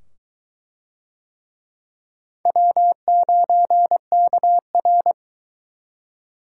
Morse code, heard as a keyed tone, about 20 wpm, 700 Hz. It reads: W9KR